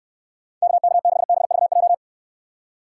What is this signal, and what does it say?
Morse code, heard as a keyed tone, about 69 wpm, 700 Hz. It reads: KK6BVJ